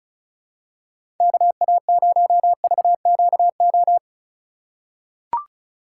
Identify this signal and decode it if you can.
Morse code, heard as a keyed tone, about 36 wpm, 700 Hz. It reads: KA0VQO
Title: USA Call Signs - 6 Characters - 2x Character Spacing - 35wpm